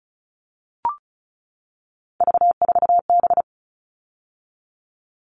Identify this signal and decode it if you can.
Morse code, heard as a keyed tone, about 34 wpm, 700 Hz. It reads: V4B
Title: All Cut Numbers - 3-Digit 35wpm